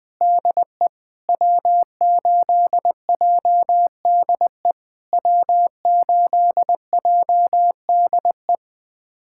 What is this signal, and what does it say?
Morse code, heard as a keyed tone, about 20 wpm, 700 Hz. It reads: DE W8JDE W8JDE
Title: POTA QSO Practice - 20wpm